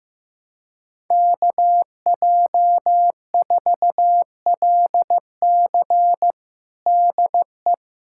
Morse code, heard as a keyed tone, about 15 wpm, 700 Hz. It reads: KJ4LC DE
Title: USA Callsign DE Callsign - 5 Characters - 15wpm